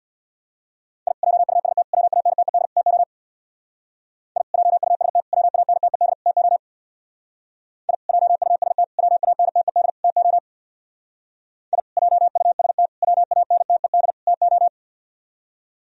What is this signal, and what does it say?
Morse code, heard as a keyed tone, about 59 wpm, 700 Hz. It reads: I JUST WANTED TO I JUST WANTED TO I JUST WANTED TO I JUST WANTED TO
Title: Speed-Racing - Sets of 4 Words 50wpm